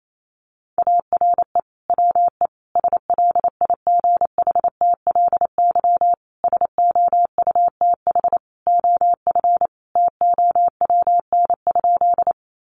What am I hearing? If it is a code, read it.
Morse code, heard as a keyed tone, about 28 wpm, 700 Hz. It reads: ARE WE SLIGHTLY SOUTH OF TOWN?
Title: Sentences from Top 1300 Words 28wpm